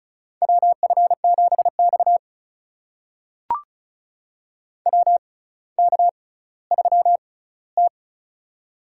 Morse code, heard as a keyed tone, about 36 wpm, spaced as for 18 wpm, 700 Hz. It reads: WF7X WK3T
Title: USA Call Signs - 4 Characters - 6x Character Spacing - 35wpm